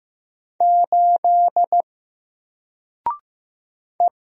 Morse code, heard as a keyed tone, about 15 wpm, 700 Hz. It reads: 8 E
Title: Single Letter-Number 15wpm